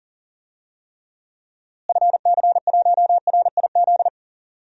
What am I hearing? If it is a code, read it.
Morse code, heard as a keyed tone, about 40 wpm, 700 Hz. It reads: FC1RIZ
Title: DXCC-100 Call Signs - 6 Characters - Repeated 2x - 40wpm